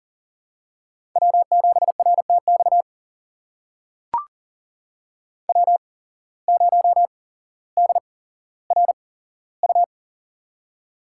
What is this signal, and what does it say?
Morse code, heard as a keyed tone, about 40 wpm, spaced as for 17 wpm, 700 Hz. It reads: W7RTX W0DRU
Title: USA Call Signs - 5 Characters - 8x Character Spacing - 40wpm